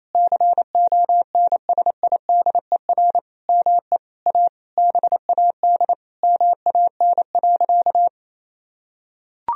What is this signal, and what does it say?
Morse code, heard as a keyed tone, about 28 wpm, 700 Hz. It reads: CONSIDER ME A BAD MAN.
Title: Sentences from Top 1000 Words - Review Part 4 - 28wpm